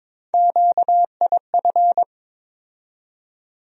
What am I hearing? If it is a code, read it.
Morse code, heard as a keyed tone, about 22 wpm, 700 Hz. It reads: QIF
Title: USA Call Sign Postfixes - 3 Letters 22wpm